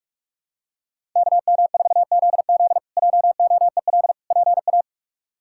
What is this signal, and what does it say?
Morse code, heard as a keyed tone, about 41 wpm, 700 Hz. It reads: KM4ZZ JOEL PA
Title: K1USN SST QSO Practice - 45wpm